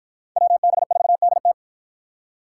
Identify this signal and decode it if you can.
Morse code, heard as a keyed tone, about 53 wpm, 700 Hz. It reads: WB4DT